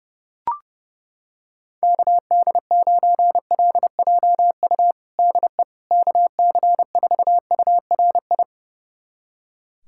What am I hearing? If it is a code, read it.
Morse code, heard as a keyed tone, about 30 wpm, 700 Hz. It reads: KD9LJU DE KC4URI
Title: USA Callsign DE Callsign 30wpm